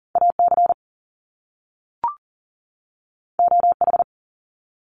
Morse code, heard as a keyed tone, about 39 wpm, 700 Hz. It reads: AC OH
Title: Top Two Letter Combinations in Words 40wpm